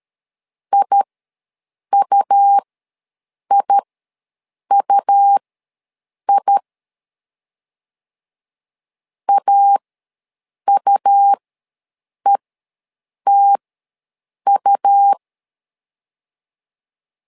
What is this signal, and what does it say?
Morse code, heard as a keyed tone, about 13 wpm, spaced as for 7 wpm, 800 Hz. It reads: IUIUI AUETU
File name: LEZIONI / 2 UHDN / 2 U ws 12wpm 1655